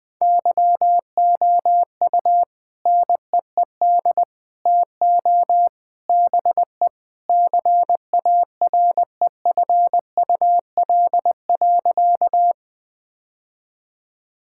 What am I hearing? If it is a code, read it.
Morse code, heard as a keyed tone, about 20 wpm, 700 Hz. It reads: YOU NEED TO BE CAREFUL.
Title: Sentences with 3-Word Phrases - 20wpm